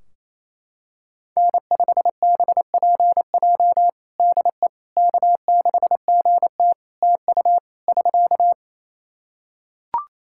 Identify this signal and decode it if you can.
Morse code, heard as a keyed tone, about 28 wpm, 700 Hz. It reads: N5BPJ DE K6GT TU <SK>
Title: K1USN SST QSO Practice 28wpm